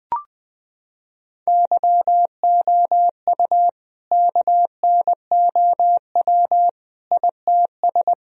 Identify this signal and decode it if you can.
Morse code, heard as a keyed tone, about 21 wpm, 700 Hz. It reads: YOU KNOW ITS
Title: Sentences from Top 1000 Words - Review Part 4 - 20wpm